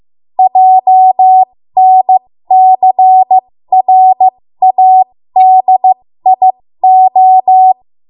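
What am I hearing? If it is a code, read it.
Morse code, heard as a keyed tone, about 15 wpm, spaced as for 13 wpm, 750 Hz. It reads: JNCRADIO
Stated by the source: 13 wpm Morse code practice archive transmission.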